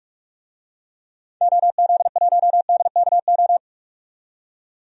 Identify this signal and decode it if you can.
Morse code, heard as a keyed tone, about 42 wpm, 700 Hz. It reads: OZ1DKO
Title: DXCC-100 Call Signs - 6 Characters - 45wpm